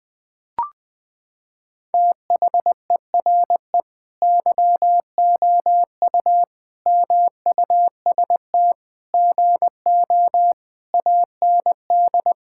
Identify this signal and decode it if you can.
Morse code, heard as a keyed tone, about 21 wpm, 700 Hz. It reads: THERE YOU MUST GO AND
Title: Sentences from Top 1600 Words 20wpm